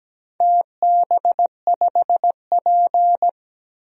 Morse code, heard as a keyed tone, about 17 wpm, 700 Hz. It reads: TB5P
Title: DXCC-100 Call Signs - 4 Characters - 17wpm